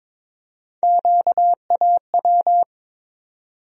Morse code, heard as a keyed tone, about 22 wpm, 700 Hz. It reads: QAW
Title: USA Call Sign Postfixes - 3 Letters 22wpm